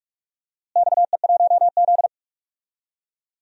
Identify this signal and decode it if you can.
Morse code, heard as a keyed tone, about 46 wpm, 700 Hz. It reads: XE1Z
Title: DXCC-100 Call Signs - 4 Characters - Repeated 2x - 45wpm